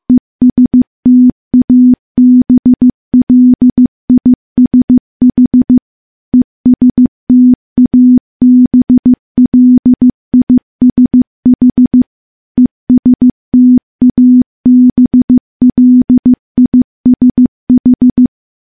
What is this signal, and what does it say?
Morse code, heard as a keyed tone, about 15 wpm, 250 Hz. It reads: ESTABLISH ESTABLISH ESTABLISH
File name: morse / morse-training-15WPM